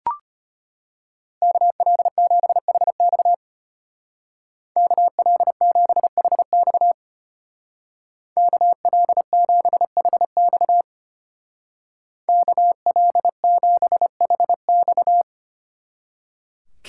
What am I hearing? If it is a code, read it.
Morse code, heard as a keyed tone, about 32 wpm, 700 Hz. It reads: KL7HX KL7HX KL7HX KL7HX